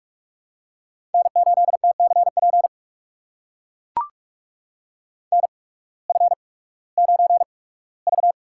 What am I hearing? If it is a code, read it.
Morse code, heard as a keyed tone, about 44 wpm, spaced as for 21 wpm, 700 Hz. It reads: N8TCP NF9V
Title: USA Call Signs - 8x Character Spacing - 45wpm